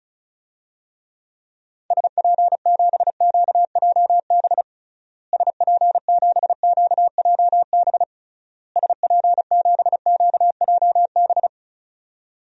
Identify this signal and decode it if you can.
Morse code, heard as a keyed tone, about 35 wpm, 700 Hz. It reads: SP7QJB SP7QJB SP7QJB
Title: DXCC-100 Call Signs - 6 Characters - Repeated 3x - 35wpm